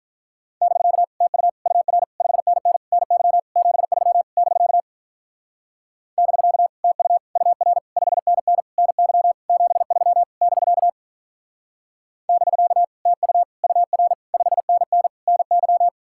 Morse code, heard as a keyed tone, about 46 wpm, 700 Hz. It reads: <BK> TU UR 5NN NY 73 <BK> <BK> TU UR 5NN NY 73 <BK> <BK> TU UR 5NN NY